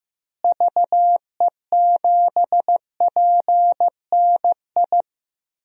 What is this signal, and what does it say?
Morse code, heard as a keyed tone, about 15 wpm, 700 Hz. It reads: VE7PNI
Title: DXCC-100 Call Signs - 6 Characters - Repeated 3x - 15wpm